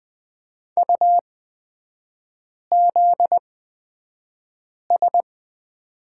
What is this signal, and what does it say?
Morse code, heard as a keyed tone, about 20 wpm, spaced as for 5 wpm, 700 Hz. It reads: UZS
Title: 255 Lesson 41e - Period - Three Characters 20wpm